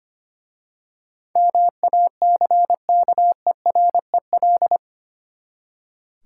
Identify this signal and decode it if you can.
Morse code, heard as a keyed tone, about 25 wpm, 700 Hz. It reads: MACKEREL
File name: practice / 25wpm / Easy Words with Hints - 25wpm